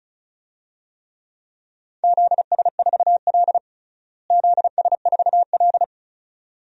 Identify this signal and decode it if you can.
Morse code, heard as a keyed tone, about 34 wpm, 700 Hz. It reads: ZS4L ZS4L
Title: DXCC-100 Call Signs - 4 Characters - Repeated 2x - 35wpm